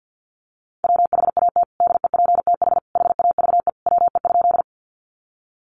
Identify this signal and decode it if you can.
Morse code, heard as a keyed tone, about 50 wpm, 700 Hz. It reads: WHAT DEPTH HAVE WE?
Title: Sentences from Top 1200 Words 50wpm